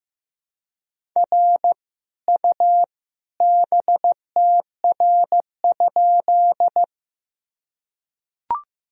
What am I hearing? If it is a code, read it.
Morse code, heard as a keyed tone, about 15 wpm, 700 Hz. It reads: R U BTR?
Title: Sentences from Top 400 Words and Abbreviations 15wpm